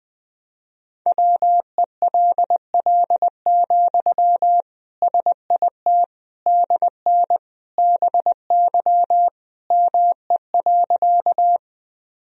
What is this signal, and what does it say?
Morse code, heard as a keyed tone, about 20 wpm, 700 Hz. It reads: WELL, SIT DN BY ME.
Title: Sentences from Top 500 Words and Abbreviations 20wpm